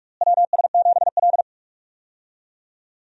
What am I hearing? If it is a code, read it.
Morse code, heard as a keyed tone, about 42 wpm, 700 Hz. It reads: WS7L
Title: USA Call Signs - 4 Characters - Repeated 2x - 45wpm